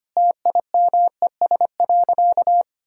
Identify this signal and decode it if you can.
Morse code, heard as a keyed tone, about 25 wpm, 700 Hz. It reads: TIMES.
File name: practice / 25wpm / Sentences with 3-Word Phrases - 25wpm